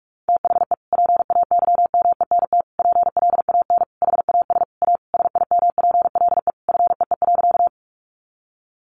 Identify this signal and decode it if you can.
Morse code, heard as a keyed tone, about 44 wpm, 700 Hz. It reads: THE PAYMENT PLAN HAS A SIMPLE FEE.
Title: Sentences from Top 1500 Words 45wpm